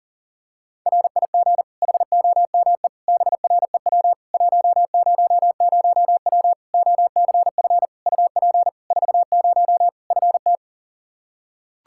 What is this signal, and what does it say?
Morse code, heard as a keyed tone, about 40 wpm, 700 Hz. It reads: RIG HOME BREW 100W OCF UP 40 FT
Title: K1USN SST QSO Practice 40wpm